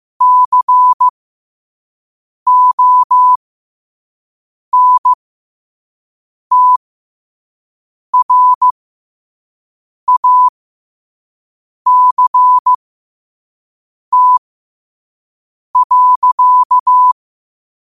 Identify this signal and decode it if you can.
Morse code, heard as a keyed tone, about 15 wpm, spaced as for 5 wpm, 1000 Hz. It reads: CONTRACT.
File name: qotdfiles / 5wpm